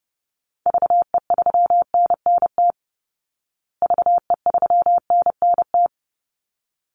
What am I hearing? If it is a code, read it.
Morse code, heard as a keyed tone, about 30 wpm, 700 Hz. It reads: VE3NNT VE3NNT
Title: Call Signs Repeated 3x - 30wpm